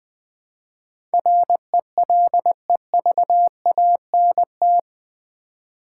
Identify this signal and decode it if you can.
Morse code, heard as a keyed tone, about 20 wpm, 700 Hz. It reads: RELEVANT
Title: Build A Word - 3 Letters At A Time - 8-Letter Words - 20wpm